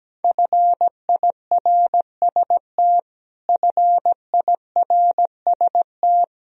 Morse code, heard as a keyed tone, about 17 wpm, 700 Hz. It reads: FIRST FIRST
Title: Top 100 Words Repeated 17wpm